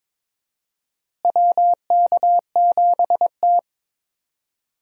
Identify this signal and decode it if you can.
Morse code, heard as a keyed tone, about 22 wpm, 700 Hz. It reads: WK7T
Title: USA Call Signs - 4 Characters - Repeated 2x - 22wpm